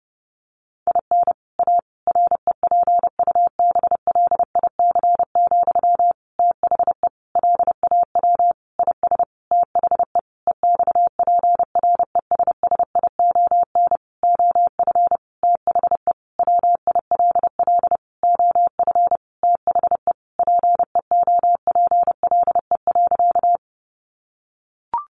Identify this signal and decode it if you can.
Morse code, heard as a keyed tone, about 30 wpm, 700 Hz. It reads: IN A REPUBLIC, THE LAW IS THE EXPRESSION OF THE WILL OF THE PEOPLE.
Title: Sentences from Top 1500 Words 30wpm